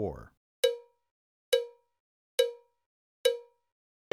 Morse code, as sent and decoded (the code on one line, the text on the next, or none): none